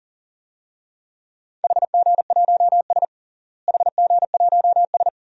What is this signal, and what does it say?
Morse code, heard as a keyed tone, about 37 wpm, 700 Hz. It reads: HG1S HG1S
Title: DXCC-100 Call Signs - 4 Characters - Repeated 2x - 40wpm